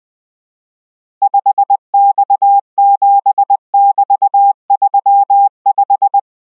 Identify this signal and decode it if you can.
Morse code, heard as a keyed tone, about 20 wpm, 800 Hz. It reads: 5X7<BT>35